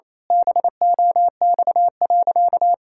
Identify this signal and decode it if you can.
Morse code, heard as a keyed tone, about 28 wpm, 700 Hz. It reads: BOX.
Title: Pangram Challenge v2 - 28wpm